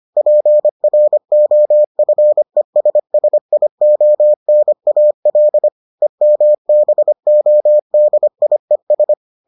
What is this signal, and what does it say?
Morse code, heard as a keyed tone, about 25 wpm, 600 Hz. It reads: PROFESSIONAL EMBODIES